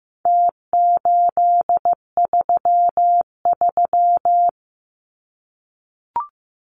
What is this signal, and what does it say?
Morse code, heard as a keyed tone, about 15 wpm, 700 Hz. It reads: T833
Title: Numbers - Contest Style - 4-Digit 15wpm